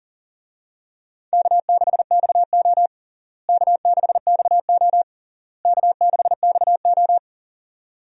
Morse code, heard as a keyed tone, about 40 wpm, 700 Hz. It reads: K6XO K6XO K6XO
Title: USA Call Signs - 4 Characters - Repeated 3x - 40wpm